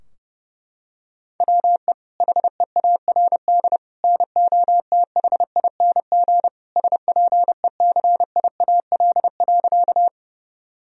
Morse code, heard as a keyed tone, about 32 wpm, 700 Hz. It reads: WE HEARD NOTHING SPECIAL.
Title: Sentences from Top 1000 Words - Review Part 3 - 30wpm